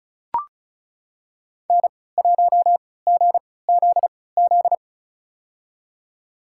Morse code, heard as a keyed tone, about 34 wpm, spaced as for 20 wpm, 700 Hz. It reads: N1GZZ
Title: USA Call Signs - 5 Characters - 3x Character Spacing - 35wpm